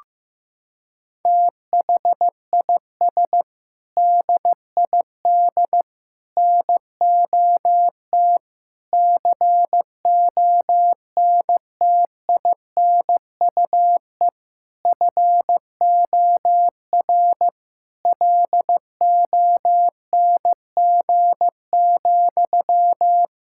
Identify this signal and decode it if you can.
Morse code, heard as a keyed tone, about 15 wpm, 700 Hz. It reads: THIS DID NOT CONTINUE FOR LONG,
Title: Sentences from Top 1000 Words - Review Part 6 - 15wpm